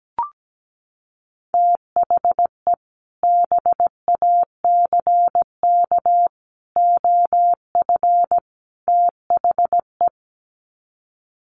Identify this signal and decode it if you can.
Morse code, heard as a keyed tone, about 17 wpm, 700 Hz. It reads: THE BACK OF THE
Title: Sets of 4 Words 17wpm